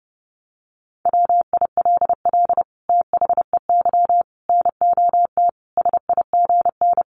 Morse code, heard as a keyed tone, about 30 wpm, 700 Hz. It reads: WILL THEY NOT SIGN
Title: Sentences from Top 1000 Words 30wpm